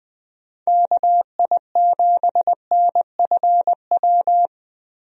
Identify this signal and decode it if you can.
Morse code, heard as a keyed tone, about 20 wpm, 700 Hz. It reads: KI7NFW
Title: USA Call Signs - 6 Characters - Repeated 2x - 20wpm